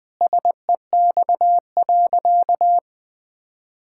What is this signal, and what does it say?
Morse code, heard as a keyed tone, about 20 wpm, 700 Hz. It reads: SEX.
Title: Sentences from Top 1600 Words 20wpm